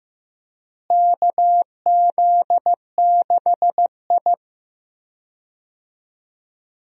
Morse code, heard as a keyed tone, about 15 wpm, 700 Hz. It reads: KZ6I
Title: USA Call Signs - 4 Characters - Repeated 3x - 15wpm